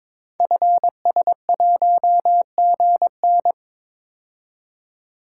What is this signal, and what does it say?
Morse code, heard as a keyed tone, about 22 wpm, 700 Hz. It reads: FS1GN
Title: Code Groups - 5 Letter:Numbers - 22wpm